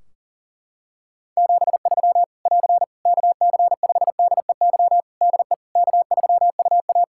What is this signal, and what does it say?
Morse code, heard as a keyed tone, about 37 wpm, 700 Hz. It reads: 73 <AR> KC5DEY DE K3UA
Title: Practice QSO - Exchange 4 - Ending - 40wpm